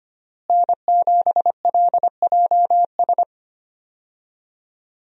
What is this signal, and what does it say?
Morse code, heard as a keyed tone, about 25 wpm, 700 Hz. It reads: N7LJS
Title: USA Call Signs - 5 Characters - Repeated 2x - 25wpm